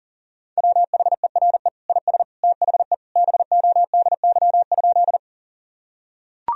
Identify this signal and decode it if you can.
Morse code, heard as a keyed tone, about 40 wpm, 700 Hz. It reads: WHERE IS THE BODY?
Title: Sentences from Top 1000 Words - Review Part 2 - 40wpm